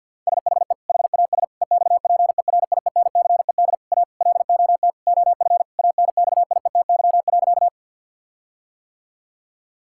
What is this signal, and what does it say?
Morse code, heard as a keyed tone, about 50 wpm, 700 Hz. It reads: SHE HAS EXPERIENCED A LOT OF ANXIETY.
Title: Sentences from Top 1600 Words 50wpm